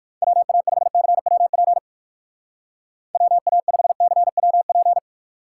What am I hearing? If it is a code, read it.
Morse code, heard as a keyed tone, about 58 wpm, 700 Hz. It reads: WA5CWP WA5CWP